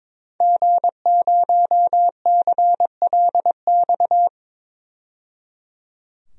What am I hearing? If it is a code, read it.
Morse code, heard as a keyed tone, about 22 wpm, 700 Hz. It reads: G0CLX
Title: DXCC-100 Call Signs - 5 Characters - 22wpm